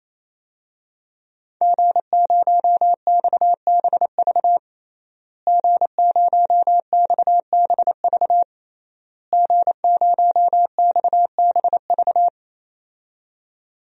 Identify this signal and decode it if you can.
Morse code, heard as a keyed tone, about 28 wpm, 700 Hz. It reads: G0XBV G0XBV G0XBV
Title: DXCC-100 Call Signs - 5 Characters - Repeated 3x - 28wpm